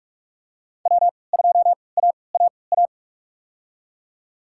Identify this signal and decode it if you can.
Morse code, heard as a keyed tone, about 46 wpm, spaced as for 25 wpm, 700 Hz. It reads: W2AAA
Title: USA Call Signs - 3x Character Spacing - 45wpm